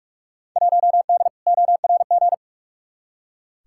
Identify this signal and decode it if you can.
Morse code, heard as a keyed tone, about 46 wpm, 700 Hz. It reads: 1D ORG
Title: Field Day Exchanges - 45wpm